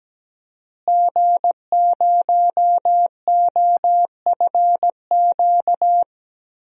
Morse code, heard as a keyed tone, about 17 wpm, 700 Hz. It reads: G0OFQ
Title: DXCC-100 Call Signs - 5 Characters - Repeated 2x - 17wpm